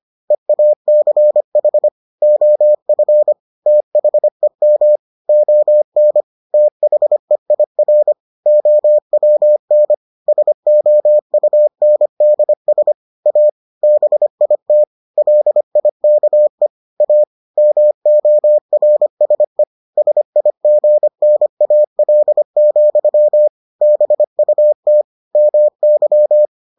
Morse code, heard as a keyed tone, about 25 wpm, 600 Hz. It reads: EACH OF THEM ON THEIR OWN SOUNDS A BIT LIKE A MORSE SIGNAL, BUT MY